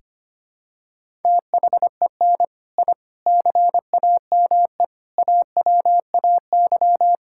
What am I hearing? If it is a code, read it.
Morse code, heard as a keyed tone, about 25 wpm, 700 Hz. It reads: THEN I CAME AWAY
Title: Sentences from Top 1000 Words - Review Part 6 - 25wpm